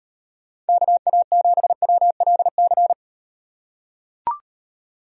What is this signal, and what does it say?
Morse code, heard as a keyed tone, about 38 wpm, 700 Hz. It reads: KA7WLC